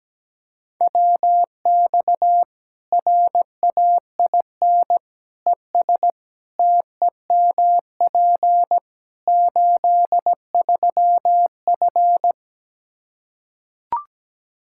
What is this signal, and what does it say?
Morse code, heard as a keyed tone, about 17 wpm, 700 Hz. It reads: WX RAIN ES TEMP 83F
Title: Practice QSO - Exchange 2 - Info - 17wpm